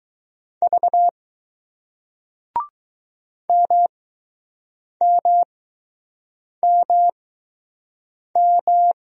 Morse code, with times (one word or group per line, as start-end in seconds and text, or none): 0.6-1.1 V
3.5-8.9 MMMM